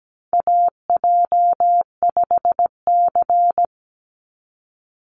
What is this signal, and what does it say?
Morse code, heard as a keyed tone, about 17 wpm, 700 Hz. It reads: AJ5C
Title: NA SOTA Chaser Callsigns 17wpm